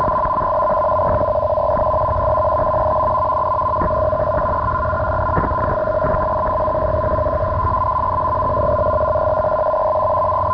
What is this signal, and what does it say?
Two-tone teleprinter signal, about 400 Hz shift.
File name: arqm2